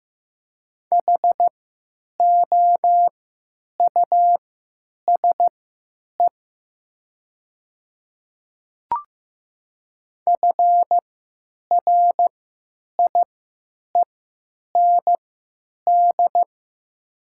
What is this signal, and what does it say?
Morse code, heard as a keyed tone, about 15 wpm, spaced as for 9 wpm, 700 Hz. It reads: HOUSE FRIEND
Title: 3x Character Spacing - Top 300 Words - 15wpm